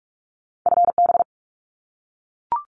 Morse code, heard as a keyed tone, about 44 wpm, 700 Hz. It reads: FB
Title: Common QSO Phrases 45wpm